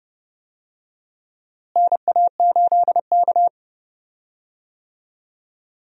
Morse code, heard as a keyed tone, about 30 wpm, 700 Hz. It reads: NA8K
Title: USA Call Signs - 4 Characters - Repeated 3x - 30wpm